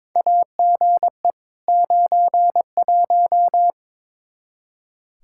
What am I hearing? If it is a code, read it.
Morse code, heard as a keyed tone, about 21 wpm, 700 Hz. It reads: AGE 91
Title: Practice QSO - Exchange 3 - More Info - 22wpm